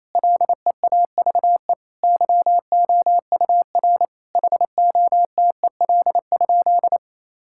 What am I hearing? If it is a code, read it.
Morse code, heard as a keyed tone, about 28 wpm, 700 Hz. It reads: LEAVE YOUR HOTEL?
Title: Sentences from Top 1600 Words 28wpm